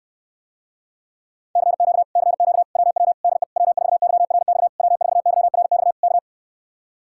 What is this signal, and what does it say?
Morse code, heard as a keyed tone, about 69 wpm, 700 Hz. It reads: CQ CQ WW DE W4YRX W4YRX K